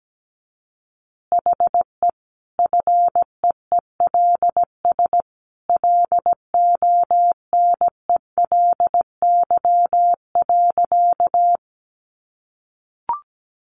Morse code, heard as a keyed tone, about 17 wpm, 700 Hz. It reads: HE FEELS LONELY.
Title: Easy Phrases 17wpm